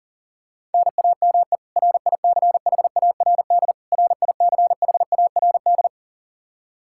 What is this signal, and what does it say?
Morse code, heard as a keyed tone, about 37 wpm, 700 Hz. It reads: NAME RICHARD RICHARD
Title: Practice QSO - Exchange 1 - Beginning - 40wpm